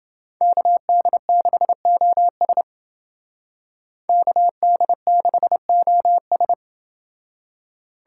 Morse code, heard as a keyed tone, about 28 wpm, 700 Hz. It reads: KD6OS KD6OS